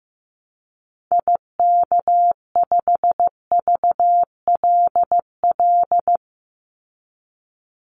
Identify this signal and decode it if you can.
Morse code, heard as a keyed tone, about 15 wpm, 700 Hz. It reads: IK5VLL
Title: Call Signs Repeated 2x - 15wpm